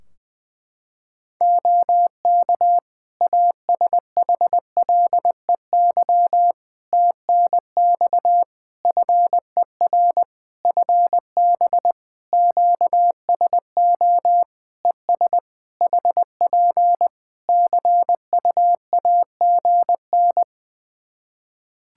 Morse code, heard as a keyed tone, about 20 wpm, 700 Hz. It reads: OK ASHLEY TNX FER FB QSO ES HP CUAGN
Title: Practice QSO - Exchange 4 - Ending - 20wpm